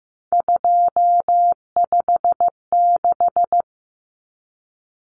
Morse code, heard as a keyed tone, about 15 wpm, 700 Hz. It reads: 256
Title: Common Cut Numbers - 3-Digit 15wpm